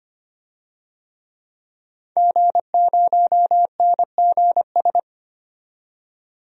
Morse code, heard as a keyed tone, about 25 wpm, 700 Hz. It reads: G0NGS
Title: DXCC-100 Call Signs - 5 Characters - 25wpm